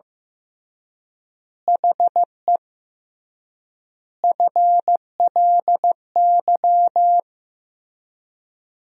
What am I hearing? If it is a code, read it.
Morse code, heard as a keyed tone, about 14 wpm, 700 Hz. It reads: HE FLY
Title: 3x Word-Spacing - Sentences from Top 500 Words - 15wpm